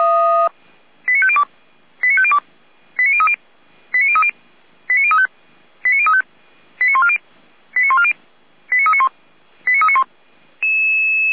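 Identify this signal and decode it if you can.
Five-tone selective call (ZVEI), heard as quick runs of five tones, tones of about 70 ms. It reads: 89502 89502 89030 89030 89035 89035 89140 89140 89391 89391
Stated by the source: ZVEI five-tone selective call received by SDR, code 89042.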